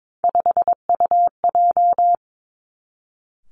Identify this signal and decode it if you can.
Morse code, heard as a keyed tone, about 22 wpm, 700 Hz. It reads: 5UJ
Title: Three Challenging Letter-Number Combinations 22wpm